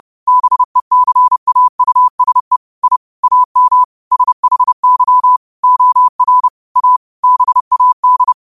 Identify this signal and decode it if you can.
Morse code, heard as a keyed tone, about 30 wpm, 1000 Hz. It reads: BECAUSE I AM SHY OR A BAD
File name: qotdfiles / 30wpm